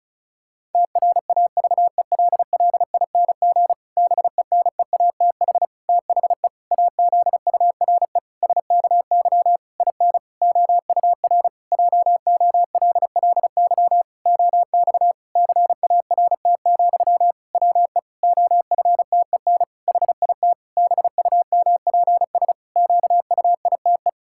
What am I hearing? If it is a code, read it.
Morse code, heard as a keyed tone, about 34 wpm, 700 Hz. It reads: TRAVELLING BENEATH THE AZURE SKY IN OUR JOLLY OX CART, WE OFTEN HIT BUMPS QUITE
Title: Pangram Challenge v2 - 35wpm